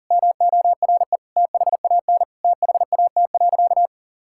Morse code, heard as a keyed tone, about 40 wpm, 700 Hz. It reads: MORE THAN THAT.
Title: Sentences from Top 1600 Words 40wpm